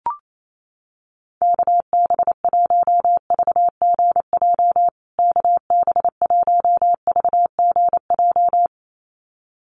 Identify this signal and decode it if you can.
Morse code, heard as a keyed tone, about 28 wpm, 700 Hz. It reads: KB1VGJ KB1VGJ
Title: USA Call Signs Repeated 2x 28wpm